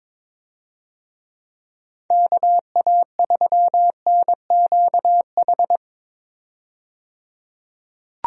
Morse code, heard as a keyed tone, about 22 wpm, 700 Hz. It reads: KA3NQH
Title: USA Call Signs - 6 Characters - 22wpm